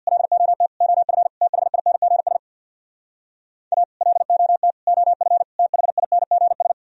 Morse code, heard as a keyed tone, about 64 wpm, 700 Hz. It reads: LOT OF THINGS A LOT OF THINGS